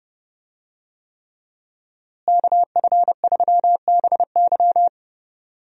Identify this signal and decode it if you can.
Morse code, heard as a keyed tone, about 32 wpm, 700 Hz. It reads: KF3BY
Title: USA Call Signs - 5 Characters - Repeated 3x - 30wpm